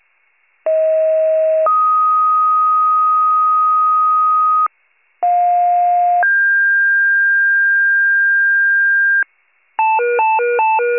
Two-tone page, heard as a one-second tone, then a three-second tone, about 640 then 1250 Hz, about 710 then 1600 Hz, with noise. It is heard from 0.7 to 9.2 s.